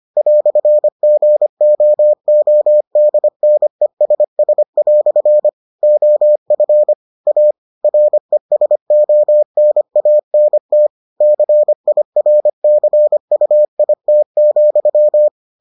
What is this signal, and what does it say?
Morse code, heard as a keyed tone, about 25 wpm, 600 Hz. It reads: "GOODNESS" OF A RESONANT CIRCUIT,